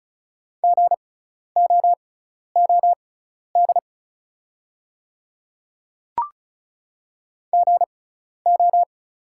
Morse code, heard as a keyed tone, about 34 wpm, spaced as for 12 wpm, 690 Hz. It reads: GOOD GO
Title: 6x Character Spacing - Top 200 Words - 35wpm